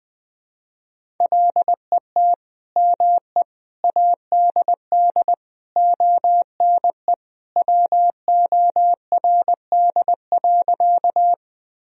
Morse code, heard as a keyed tone, about 20 wpm, 700 Hz. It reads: LET ME ADD ONE WORD.
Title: Sentences from Top 1000 Words - Review Part 1 - 20wpm